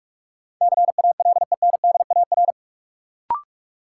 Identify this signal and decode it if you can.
Morse code, heard as a keyed tone, about 46 wpm, 700 Hz. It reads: CALENDAR
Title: Sentences from Top 1600 Words - New Word Drill - 45wpm